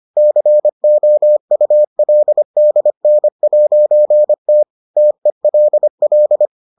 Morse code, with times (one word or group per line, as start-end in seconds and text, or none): 0.2-4.6 COULDN'T
5.0-6.5 TELL